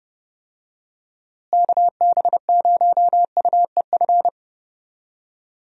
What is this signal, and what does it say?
Morse code, heard as a keyed tone, about 30 wpm, 700 Hz. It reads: KB0UEF
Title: USA Call Signs - 6 Characters - Repeated 2x - 30wpm